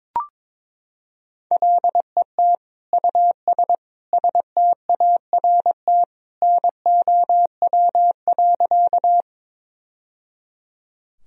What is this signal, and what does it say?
Morse code, heard as a keyed tone, about 22 wpm, 700 Hz. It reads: LET US START NOW.
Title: Sentences from Top 1000 Words - Review Part 1 - 22wpm